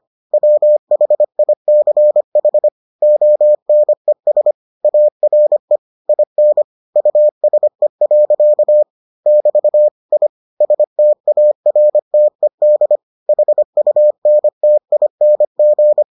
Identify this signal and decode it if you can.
Morse code, heard as a keyed tone, about 25 wpm, 600 Hz. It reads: WHICH ONES ARE IN USE. <BT> I STARTED HUNTING